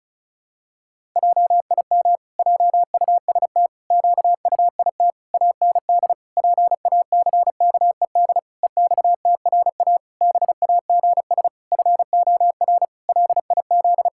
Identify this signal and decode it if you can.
Morse code, heard as a keyed tone, about 34 wpm, 700 Hz. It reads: JIM JUST QUIT AND PACKED EXTRA BAGS FOR LIZ
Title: Pangram Challenge v2 - 35wpm